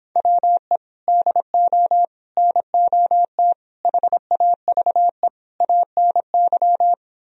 Morse code, heard as a keyed tone, about 26 wpm, 700 Hz. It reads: WE DO NOT HAVE ANY